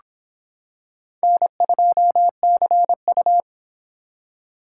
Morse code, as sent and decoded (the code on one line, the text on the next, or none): -. ..--- -.-. ..-
N2CU